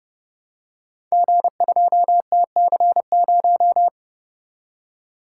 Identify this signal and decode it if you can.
Morse code, heard as a keyed tone, about 30 wpm, 700 Hz. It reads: G2TC0
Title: Code Groups - 5 Letter:Numbers - 30wpm